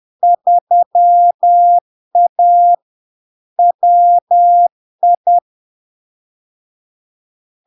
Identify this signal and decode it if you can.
Morse code, heard as a keyed tone, about 10 wpm, 700 Hz. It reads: 3A WI
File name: LARK-FD-2022-Log-in-CW-10wpm-700-Hz